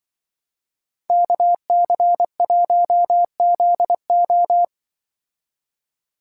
Morse code, heard as a keyed tone, about 23 wpm, 700 Hz. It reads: KC1ZO
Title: Speed Racing - USA 2x2 Call Signs 20wpm